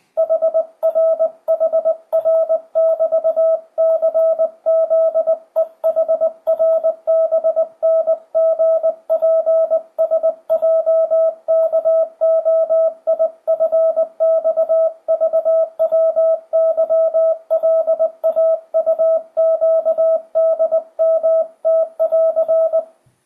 Morse code, heard as a keyed tone, about 20 wpm, 660 Hz, with radio noise. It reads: HRHR<BT>CZEHRBNGPSJKOIFXVWYLAUQDMT<AR>
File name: 2025cw / speedup / 20wpm